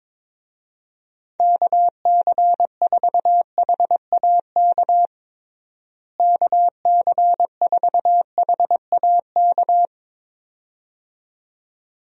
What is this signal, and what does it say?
Morse code, heard as a keyed tone, about 22 wpm, 700 Hz. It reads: KC4HAK KC4HAK
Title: USA Call Signs - 6 Characters - Repeated 2x - 22wpm